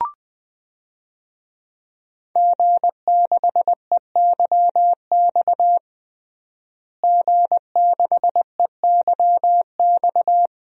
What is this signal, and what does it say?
Morse code, heard as a keyed tone, about 20 wpm, 700 Hz. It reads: G6EYX G6EYX
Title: DXCC-100 Call Signs - 5 Characters - Repeated 3x - 20wpm